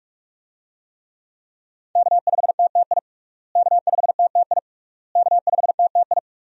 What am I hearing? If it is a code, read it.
Morse code, heard as a keyed tone, about 46 wpm, 700 Hz. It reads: K5TTI K5TTI K5TTI
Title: USA Call Signs - 5 Characters - Repeated 3x - 45wpm